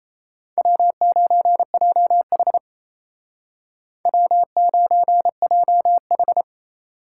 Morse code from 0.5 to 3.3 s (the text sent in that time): W9JH